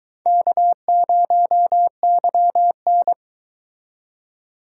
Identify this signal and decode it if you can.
Morse code, heard as a keyed tone, about 22 wpm, 700 Hz. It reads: K0YN